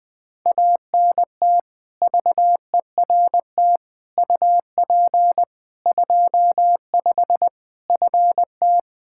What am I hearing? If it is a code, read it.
Morse code, heard as a keyed tone, about 20 wpm, 700 Hz. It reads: ANT VERT UP 25 FT
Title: K1USN SST QSO Practice 20wpm